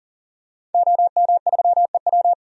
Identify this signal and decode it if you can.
Morse code, heard as a keyed tone, about 40 wpm, 700 Hz. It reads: OM3EW
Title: DXCC-100 Call Signs - 5 Characters - 40wpm